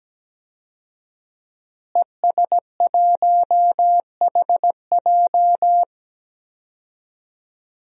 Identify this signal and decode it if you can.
Morse code, heard as a keyed tone, about 17 wpm, 700 Hz. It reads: ES1HJ
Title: DXCC-100 Call Signs - 5 Characters - Repeated 3x - 17wpm